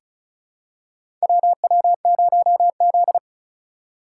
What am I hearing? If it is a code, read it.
Morse code, heard as a keyed tone, about 36 wpm, 700 Hz. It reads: WW0Z
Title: USA Call Signs - 4 Characters - 2x Character Spacing - 35wpm